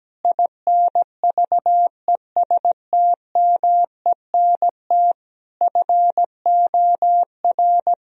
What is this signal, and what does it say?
Morse code, heard as a keyed tone, about 17 wpm, 700 Hz. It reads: INVESTMENT FOR
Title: Sentences from Top 1000 Words - Review Part 4 - 17wpm